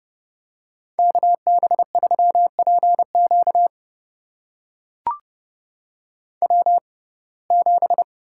Morse code, heard as a keyed tone, about 30 wpm, spaced as for 22 wpm, 700 Hz. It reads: KB3PQ W7
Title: USA Call Signs - 5 Characters - 6x Character Spacing - 30wpm